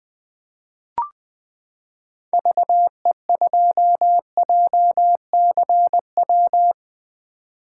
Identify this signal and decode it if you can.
Morse code, heard as a keyed tone, about 20 wpm, 700 Hz. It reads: VE2JCW
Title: 2023 SOTA Chaser Call Signs - 20wpm